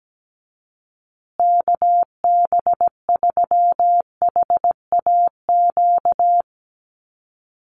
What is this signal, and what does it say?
Morse code, heard as a keyed tone, about 17 wpm, 700 Hz. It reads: KB3HAQ
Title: USA Call Signs Repeated 2x 17wpm